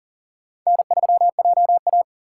Morse code, heard as a keyed tone, about 51 wpm, 700 Hz. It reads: N3JA